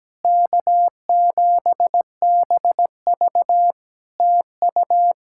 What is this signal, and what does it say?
Morse code, heard as a keyed tone, about 17 wpm, 700 Hz. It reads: K7BV TU
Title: Practice QSO - Exchange 4 - Ending - 17wpm